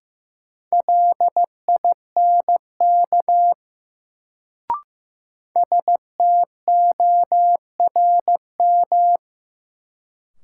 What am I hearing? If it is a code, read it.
Morse code, heard as a keyed tone, about 16 wpm, 700 Hz. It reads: LINK STORM
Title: Sentences from Top 1500 Words - New Word Drill Alt - 15wpm